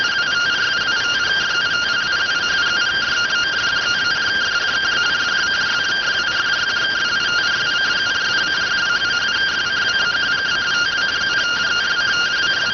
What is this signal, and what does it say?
Two-tone teleprinter signal, about 170 Hz shift.